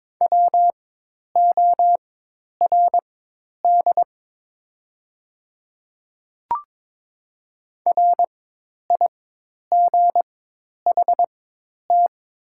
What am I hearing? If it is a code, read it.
Morse code, heard as a keyed tone, about 22 wpm, spaced as for 10 wpm, 700 Hz. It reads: WORD RIGHT
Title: 4x Character Spacing - Top 500 Words - 22wpm